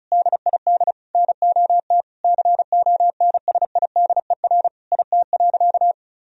Morse code, heard as a keyed tone, about 34 wpm, 700 Hz. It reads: DID NOT CONSIDER IT.
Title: Sentences from Top 1000 Words - Review Part 4 - 35wpm